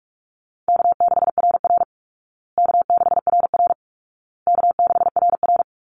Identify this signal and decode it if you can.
Morse code, heard as a keyed tone, about 44 wpm, 700 Hz. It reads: K6RR K6RR K6RR
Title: Call Signs Repeated 2x - 45wpm